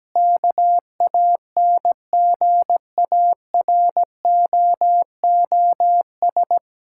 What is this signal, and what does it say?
Morse code, heard as a keyed tone, about 17 wpm, 700 Hz. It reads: KANGAROOS
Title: Pangram Challenge v2 - 17wpm